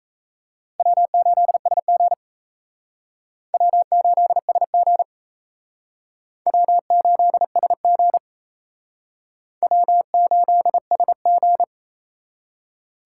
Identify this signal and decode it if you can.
Morse code, heard as a keyed tone, about 35 wpm, 700 Hz. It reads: W8SG W8SG W8SG W8SG
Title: Speed Racing - USA 1x2 Call Signs 28wpm